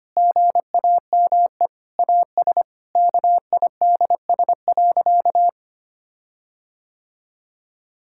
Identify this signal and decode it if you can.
Morse code, heard as a keyed tone, about 25 wpm, 700 Hz. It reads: GAME AS KIDS.
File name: practice / 25wpm / Difficult Sentences with Hints - 25wpm